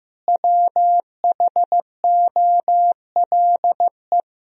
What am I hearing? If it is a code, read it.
Morse code, heard as a keyed tone, about 15 wpm, 700 Hz. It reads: WHOLE
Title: Sentences from Top 400 Words and Abbreviations 15wpm